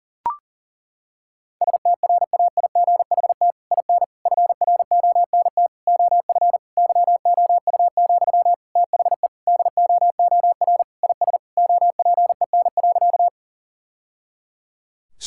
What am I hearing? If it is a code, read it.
Morse code, heard as a keyed tone, about 40 wpm, 700 Hz. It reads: STRAIGHT IN FRONT OF YOU, THE DOOR IS OPEN.
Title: Sentences from Top 1000 Words - Review Part 6 - 40wpm